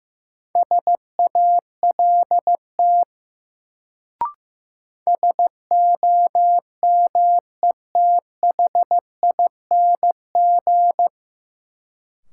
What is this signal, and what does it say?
Morse code, heard as a keyed tone, about 15 wpm, 700 Hz. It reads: SALT SOMETHING
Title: Sets of 5 Words - New Word Drill - 15wpm